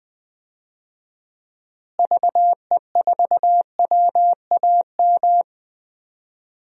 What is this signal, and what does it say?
Morse code, heard as a keyed tone, about 20 wpm, 700 Hz. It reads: VE4WAM
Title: DXCC-100 Call Signs - 6 Characters - Repeated 3x - 20wpm